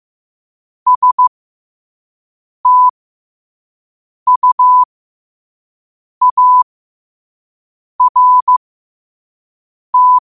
Morse code, heard as a keyed tone, about 15 wpm, spaced as for 5 wpm, 1000 Hz. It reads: STUART
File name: qotdfiles / 5wpm